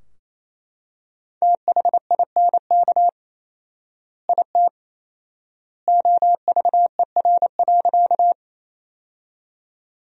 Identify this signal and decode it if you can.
Morse code, heard as a keyed tone, about 28 wpm, 700 Hz. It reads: THINK IT OVER.
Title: 4x Word-Spacing - Sentences from Top 200 Words - 28wpm